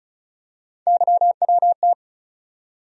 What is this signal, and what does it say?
Morse code, heard as a keyed tone, about 36 wpm, 700 Hz. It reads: YWT
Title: USA Call Sign Postfixes - 3 Letters 35wpm